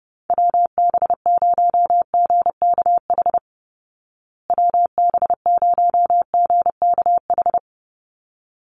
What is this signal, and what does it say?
Morse code, heard as a keyed tone, about 30 wpm, 700 Hz. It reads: WB0GKH WB0GKH
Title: Call Signs Repeated 3x - 30wpm